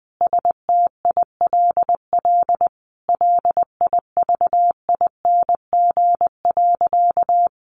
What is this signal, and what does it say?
Morse code, heard as a keyed tone, about 20 wpm, 700 Hz. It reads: STILL LIVING.
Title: Sentences from Top 1100 Words 20wpm